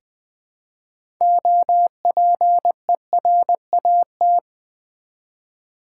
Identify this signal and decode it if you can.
Morse code, heard as a keyed tone, about 20 wpm, 700 Hz. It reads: OPERAT
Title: Build A Word - 3 Letters At A Time - 8-Letter Words - 20wpm